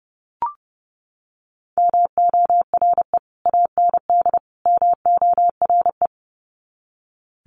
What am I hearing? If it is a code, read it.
Morse code, heard as a keyed tone, about 29 wpm, 700 Hz. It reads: MORE AND MORE
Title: Sets of 3 Words 30wpm